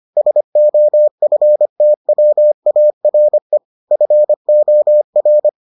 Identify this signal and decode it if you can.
Morse code, heard as a keyed tone, about 25 wpm, 600 Hz. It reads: SOFTWARE FOR